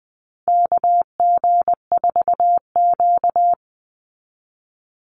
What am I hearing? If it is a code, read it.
Morse code, heard as a keyed tone, about 20 wpm, 700 Hz. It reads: KG4Q
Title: Call Signs - 20wpm